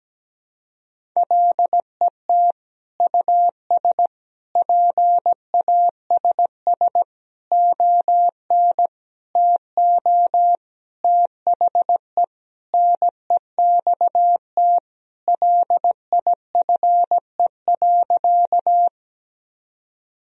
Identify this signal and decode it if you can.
Morse code, heard as a keyed tone, about 17 wpm, 700 Hz. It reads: LET US PASS ON TO THE NEXT LIFE.
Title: Sentences from Top 1000 Words - Review Part 1 - 17wpm